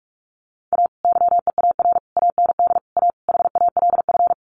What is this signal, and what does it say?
Morse code, heard as a keyed tone, about 44 wpm, 700 Hz. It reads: A YEAR AND A HALF
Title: Sets of 5 Words 45wpm